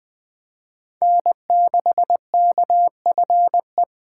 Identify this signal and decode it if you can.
Morse code, heard as a keyed tone, about 21 wpm, 700 Hz. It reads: N6KFE
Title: USA Call Signs - 6x Character Spacing - 20wpm